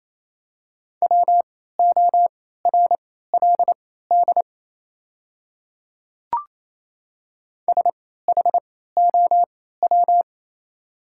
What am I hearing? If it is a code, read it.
Morse code, heard as a keyed tone, about 27 wpm, spaced as for 16 wpm, 690 Hz. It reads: WORLD SHOW
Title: 3x Character Spacing - Top 200 Words - 28wpm